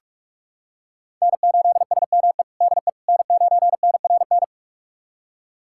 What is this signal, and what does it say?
Morse code, heard as a keyed tone, about 46 wpm, 700 Hz. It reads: N8SME DE N9NRN
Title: USA Callsign DE Callsign - 5 Characters - 45wpm